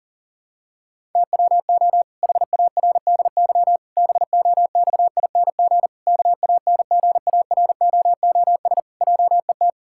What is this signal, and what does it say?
Morse code, heard as a keyed tone, about 40 wpm, 700 Hz. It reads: TWO HARDY BOXING KANGAROOS JET
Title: Pangram Challenge v2 - 40wpm